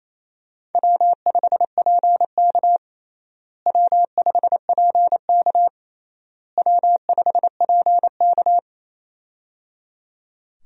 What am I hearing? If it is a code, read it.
Morse code, heard as a keyed tone, about 28 wpm, 700 Hz. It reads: W5PK W5PK W5PK
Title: USA Call Signs - 4 Characters - Repeated 3x - 28wpm